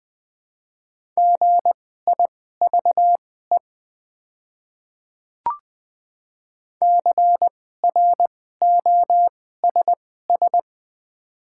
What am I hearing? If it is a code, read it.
Morse code, heard as a keyed tone, about 20 wpm, spaced as for 14 wpm, 700 Hz. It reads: GIVE CROSS
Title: 2x Character Spacing - Top 500 Words - 20wpm